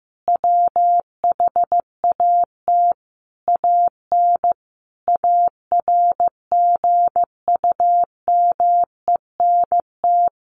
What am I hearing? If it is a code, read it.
Morse code, heard as a keyed tone, about 15 wpm, 700 Hz. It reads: WHAT AN ARGUMENT
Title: Sentences from Top 1400 Words 15wpm